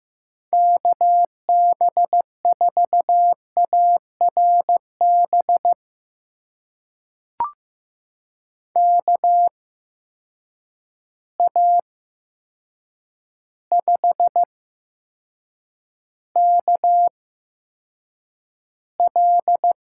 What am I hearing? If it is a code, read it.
Morse code, heard as a keyed tone, about 15 wpm, spaced as for 7 wpm, 700 Hz. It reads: KB4ARB KA5KL
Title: USA Call Signs - 6 Characters - 8x Character Spacing - 15wpm